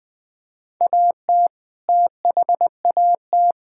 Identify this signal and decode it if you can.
Morse code, heard as a keyed tone, about 20 wpm, 700 Hz. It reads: AT THAT
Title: Sentences from Top 1000 Words - Review Part 5 - 20wpm